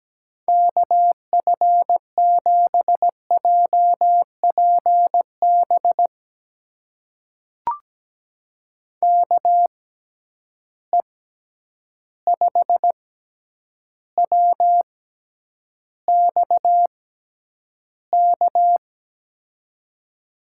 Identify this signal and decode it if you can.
Morse code, heard as a keyed tone, about 17 wpm, spaced as for 9 wpm, 700 Hz. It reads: KF7JPB KE5WXK
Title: USA Call Signs - 6 Characters - 6x Character Spacing - 17wpm